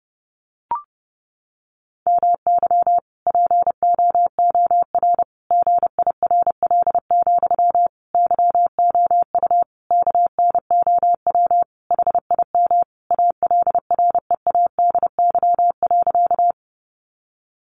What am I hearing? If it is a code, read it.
Morse code, heard as a keyed tone, about 30 wpm, 700 Hz. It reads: MY POOR GIRL, YOU KNOW HIM ALREADY.
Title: Sentences from Top 1000 Words 30wpm